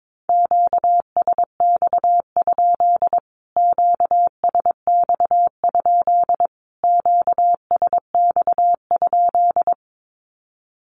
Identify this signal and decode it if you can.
Morse code, heard as a keyed tone, about 22 wpm, 700 Hz. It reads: QSX? QSX? QSX?
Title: Q-Codes Repeated 3x 22wpm